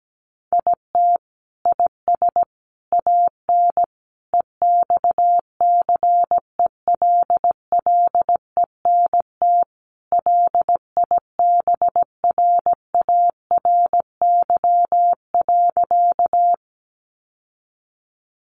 Morse code, heard as a keyed tone, about 17 wpm, 700 Hz. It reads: IT IS AN EXCELLENT LIBRARY.
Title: Sentences from Top 1000 Words 17wpm